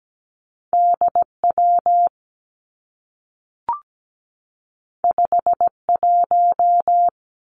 Morse code, heard as a keyed tone, about 17 wpm, 700 Hz. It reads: DW 51
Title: Two Challenging Letter-Number Combinations 17wpm